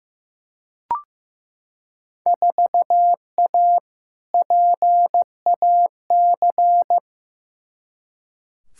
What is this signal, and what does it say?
Morse code, heard as a keyed tone, about 15 wpm, 700 Hz. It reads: 4A PAC
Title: Field Day Exchanges Encore - 15wpm